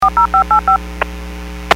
DTMF, heard as telephone touch tones, tones of about 110 ms. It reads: 40585
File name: WPS / 02104 14-34 WPS 451.25 WPS